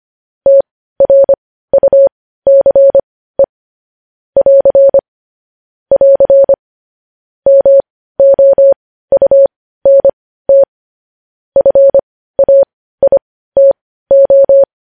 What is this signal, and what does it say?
Morse code, heard as a keyed tone, about 25 wpm, spaced as for 15 wpm, 550 Hz. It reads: TRUCE <AR> <AR> MOUNT FAITO
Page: news headlines in morse code at 15 wpm